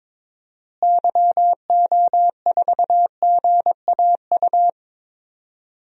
Morse code, heard as a keyed tone, about 22 wpm, 700 Hz. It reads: YO4GAU
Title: DXCC-100 Call Signs - 6 Characters - Repeated 3x - 22wpm